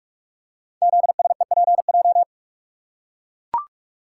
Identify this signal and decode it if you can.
Morse code, heard as a keyed tone, about 44 wpm, 700 Hz. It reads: ZSEPJ
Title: Code Groups - 5 Letters - 45wpm